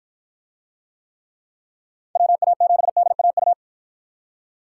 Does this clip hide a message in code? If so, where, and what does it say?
Yes: Morse code from 2.2 to 3.5 s, heard as a keyed tone, about 53 wpm, 700 Hz. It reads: WA7DAU